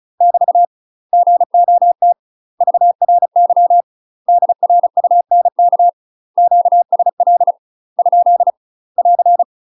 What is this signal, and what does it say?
Morse code, heard as a keyed tone, about 34 wpm, 700 Hz. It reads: <BT> GOT VRY DRUNK QSL ? <AR>
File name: UK MASTER QSOs TRAINING / 35WPM / UK MASTER QSOs 0034